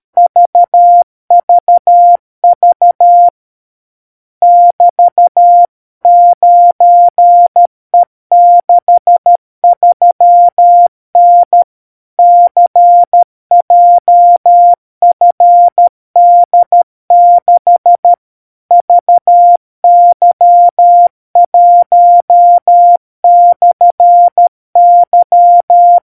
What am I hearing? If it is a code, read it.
Morse code, heard as a keyed tone, about 13 wpm, 700 Hz. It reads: VVV <BT>9E63N CJFD6 VY1/Y